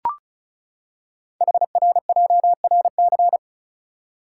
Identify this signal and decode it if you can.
Morse code, heard as a keyed tone, about 36 wpm, 690 Hz. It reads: HRJRC
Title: Code Groups - 5 Letters - 35wpm